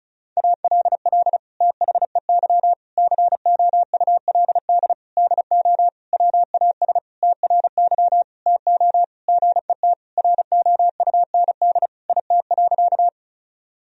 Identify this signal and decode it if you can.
Morse code, heard as a keyed tone, about 34 wpm, 700 Hz. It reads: ALL THEY COULD DO WAS TRY TO GET ROUND IT.
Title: Sentences from Top 1000 Words - Review Part 1 - 35wpm